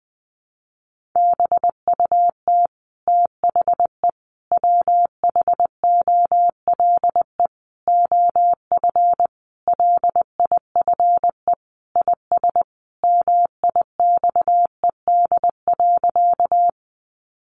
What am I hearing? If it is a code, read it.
Morse code, heard as a keyed tone, about 20 wpm, 700 Hz. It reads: BUT THE WHOLE OF LIFE IS MIXED.
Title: Sentences from Top 1500 Words 20wpm